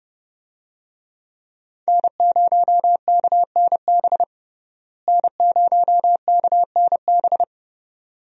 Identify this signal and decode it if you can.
Morse code, heard as a keyed tone, about 30 wpm, 700 Hz. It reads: N0KNB N0KNB
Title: USA Call Signs - 5 Characters - Repeated 2x - 30wpm